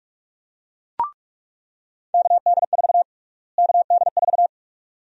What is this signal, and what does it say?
Morse code, heard as a keyed tone, about 47 wpm, 700 Hz. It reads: KD4 KD4
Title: USA Call Sign Prefixes - Repeated 2x - 45wpm